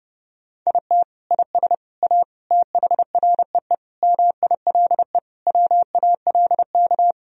Morse code, heard as a keyed tone, about 31 wpm, 700 Hz. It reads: IT IS A THREE MILE WALK
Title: Sentences from Top 1000 Words - Review Part 1 - 30wpm